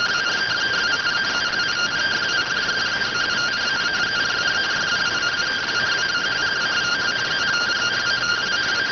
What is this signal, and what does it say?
Two-tone teleprinter signal, about 170 Hz shift.